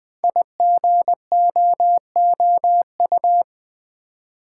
Morse code, heard as a keyed tone, about 20 wpm, 700 Hz. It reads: IGOOU
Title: Code Groups - 5 Letters - 20wpm